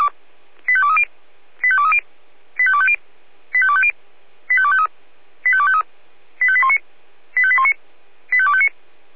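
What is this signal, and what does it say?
Five-tone selective call (ZVEI), heard as quick runs of five tones, tones of about 70 ms. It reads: 85230 85230 85250 85250 85253 85253 85819 85819 85259
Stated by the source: ZVEI five-tone selective call 85202 received on SDR.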